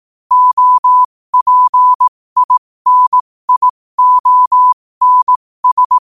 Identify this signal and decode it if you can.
Morse code, heard as a keyed tone, about 18 wpm, spaced as for 15 wpm, 1000 Hz. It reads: OPINIONS
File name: qotdfiles / 15wpm